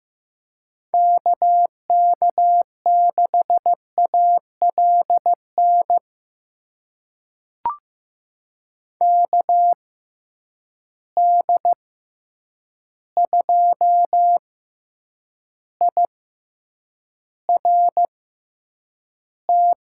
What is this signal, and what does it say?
Morse code, heard as a keyed tone, about 15 wpm, spaced as for 8 wpm, 700 Hz. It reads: KK6ALN KD2IRT
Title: USA Call Signs - 6 Characters - 6x Character Spacing - 15wpm